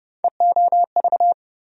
Morse code, heard as a keyed tone, about 30 wpm, 700 Hz. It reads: EOV
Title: USA Call Sign Postfixes - 3 Letters 30wpm